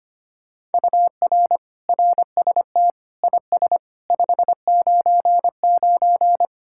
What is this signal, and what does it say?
Morse code, heard as a keyed tone, about 32 wpm, 700 Hz. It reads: UR RST IS 599